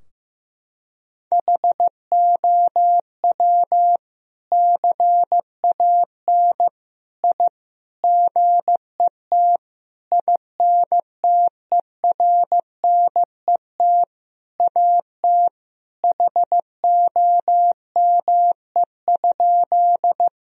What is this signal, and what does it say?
Morse code, heard as a keyed tone, about 15 wpm, 700 Hz. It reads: HOW CAN I GET INTERNET AT HOME?
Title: Sentences from Top 1000 Words - Review Part 4 - 15wpm